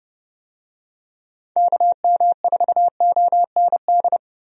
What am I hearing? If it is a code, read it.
Morse code, heard as a keyed tone, about 32 wpm, 700 Hz. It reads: KM4OND
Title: USA Call Signs - 6 Characters - 30wpm